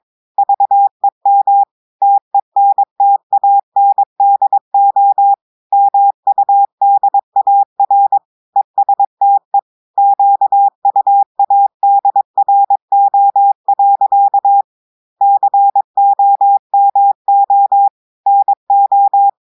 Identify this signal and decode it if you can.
Morse code, heard as a keyed tone, about 22 wpm, 800 Hz. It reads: VEM TENTANDO MUDAR ESTE QUADRO. COMO NO
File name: audios / portugais / portugais texte en clair 1 vitesse 22 mots minutes